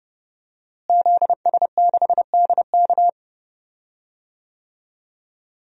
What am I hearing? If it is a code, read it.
Morse code, heard as a keyed tone, about 30 wpm, 700 Hz. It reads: ZS6DK
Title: DXCC-100 Call Signs - 5 Characters - Repeated 2x - 30wpm